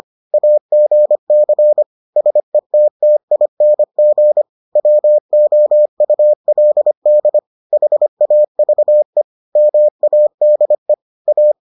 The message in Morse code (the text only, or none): AGC SETTING WOULD HAVE MADE A